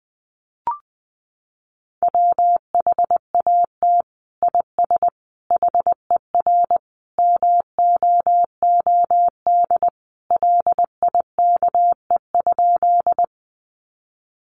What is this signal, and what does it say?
Morse code, heard as a keyed tone, about 20 wpm, 700 Hz. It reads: WHAT IS HER MOOD LIKE?
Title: Sentences from Top 1100 Words 20wpm